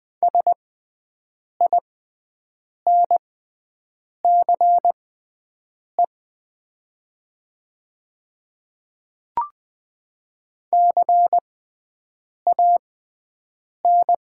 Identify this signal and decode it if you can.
Morse code, heard as a keyed tone, about 20 wpm, spaced as for 7 wpm, 700 Hz. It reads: SINCE CAN
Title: 6x Character Spacing - Top 300 Words - 20wpm